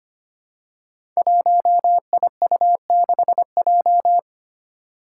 Morse code, heard as a keyed tone, about 25 wpm, 700 Hz. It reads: 1IU6J
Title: Five Challenging Letter-Number Combinations 25wpm